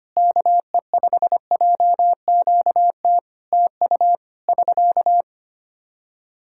Morse code, heard as a keyed tone, about 25 wpm, 700 Hz. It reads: KE5JQT TU <SK>
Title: Practice QSO - Exchange 4 - Ending - 25wpm